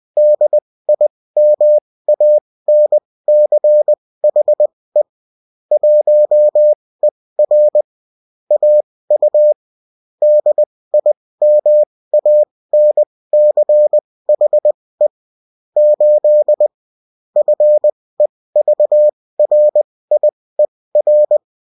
Morse code, heard as a keyed tone, about 20 wpm, spaced as for 16 wpm, 600 Hz. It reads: DIMANCHE 1ER AU DIMANCHE 8 FEVRIER